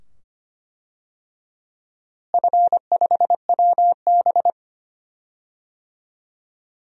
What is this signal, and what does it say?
Morse code, heard as a keyed tone, about 25 wpm, 700 Hz. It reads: F5WB
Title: DXCC-100 Call Signs - 4 Characters - Repeated 2x - 25wpm